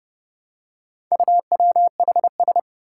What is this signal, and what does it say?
Morse code, heard as a keyed tone, about 30 wpm, 700 Hz. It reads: UWHS
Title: Four Challenging Letter-Number Combinations 30wpm